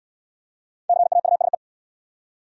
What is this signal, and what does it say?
Morse code, heard as a keyed tone, about 69 wpm, 700 Hz. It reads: CAUSE